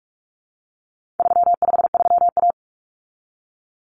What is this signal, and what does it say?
Morse code, heard as a keyed tone, about 44 wpm, 700 Hz. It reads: 353A
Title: Common Cut Numbers - 4-Digit 45wpm